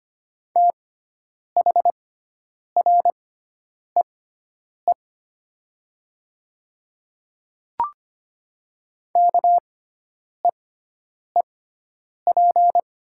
Morse code, heard as a keyed tone, about 25 wpm, spaced as for 9 wpm, 700 Hz. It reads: THREE KEEP
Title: 6x Character Spacing - Top 300 Words - 25wpm